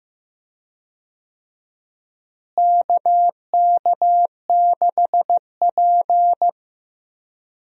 Morse code, heard as a keyed tone, about 15 wpm, 700 Hz. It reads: KK6P
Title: USA Call Signs - 4 Characters - Repeated 2x - 15wpm